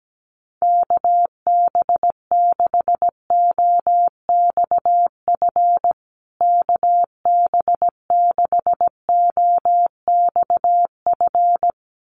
Morse code, heard as a keyed tone, about 17 wpm, 700 Hz. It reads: KB6OXF KB6OXF
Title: USA Call Signs Repeated 2x 17wpm